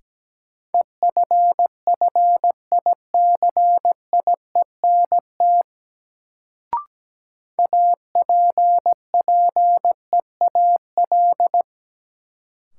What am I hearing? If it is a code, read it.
Morse code, heard as a keyed tone, about 17 wpm, 700 Hz. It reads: EFFICIENT APPEAL
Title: Sentences from Top 1300 Words - New Word Drill Alt - 17wpm